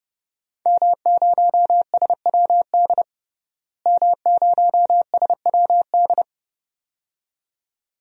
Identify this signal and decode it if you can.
Morse code, heard as a keyed tone, about 30 wpm, 700 Hz. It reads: M0SWD M0SWD
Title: DXCC-100 Call Signs - 5 Characters - Repeated 2x - 30wpm